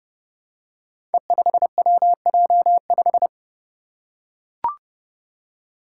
Morse code, heard as a keyed tone, about 30 wpm, 700 Hz. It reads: E5WJ5
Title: Five Challenging Letter-Number Combinations 30wpm